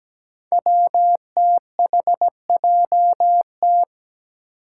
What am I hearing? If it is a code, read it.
Morse code, heard as a keyed tone, about 17 wpm, 700 Hz. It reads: WTHJT
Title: Five Challenging Letter-Number Combinations 17wpm